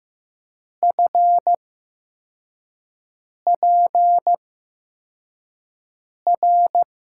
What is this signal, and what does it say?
Morse code, heard as a keyed tone, about 15 wpm, spaced as for 4 wpm, 700 Hz. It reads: FPR
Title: USA Call Signs - 6 Characters - 8x Character Spacing - 15wpm